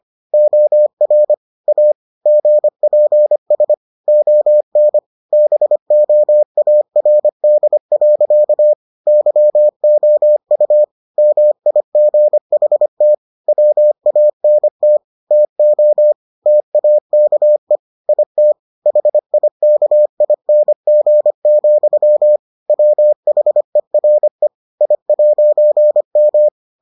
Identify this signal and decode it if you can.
Morse code, heard as a keyed tone, about 25 wpm, 600 Hz. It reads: OR A GPS ON BOARD. YOU MIGHT WANT TO TAKE IT HIKING, WHERE I'M